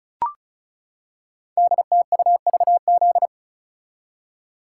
Morse code, heard as a keyed tone, about 36 wpm, 700 Hz. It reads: DTUVZ
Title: Code Groups - 5 Letters - 35wpm